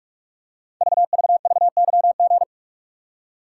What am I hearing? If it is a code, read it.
Morse code, heard as a keyed tone, about 44 wpm, 690 Hz. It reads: VVVYG
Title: Code Groups - 5 Letters - 45wpm